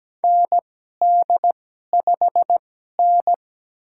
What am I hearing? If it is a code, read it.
Morse code, heard as a keyed tone, about 17 wpm, spaced as for 12 wpm, 700 Hz. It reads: ND5N
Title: USA Call Signs - 2x Character Spacing - 17wpm